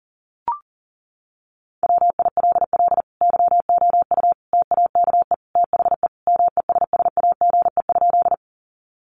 Morse code, heard as a keyed tone, about 40 wpm, 700 Hz. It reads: WILL YOU TAKE THE MESSAGE?
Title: Sentences from Top 1100 Words 40wpm